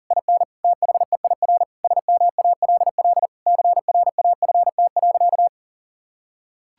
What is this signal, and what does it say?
Morse code, heard as a keyed tone, about 40 wpm, 700 Hz. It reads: IN THEIR SMALL CRAFT.
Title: Sentences from Top 1000 Words - Review Part 6 - 40wpm